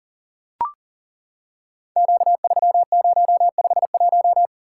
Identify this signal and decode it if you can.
Morse code, heard as a keyed tone, about 38 wpm, 690 Hz. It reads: Q3051
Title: Code Groups - 5 Letter:Numbers - 40wpm